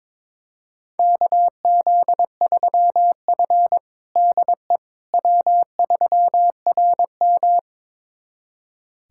Morse code, heard as a keyed tone, about 22 wpm, 700 Hz. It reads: KZ3F DE W3RM
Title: USA Callsign DE Callsign - 4 Characters - 22wpm